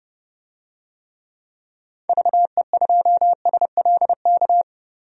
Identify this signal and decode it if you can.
Morse code, heard as a keyed tone, about 30 wpm, 700 Hz. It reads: VE2SLK
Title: DXCC-100 Call Signs - 6 Characters - 30wpm